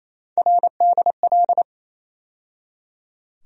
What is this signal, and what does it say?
Morse code, heard as a keyed tone, about 28 wpm, 700 Hz. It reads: RDL
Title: USA Call Sign Postfixes - 3 Letters 28wpm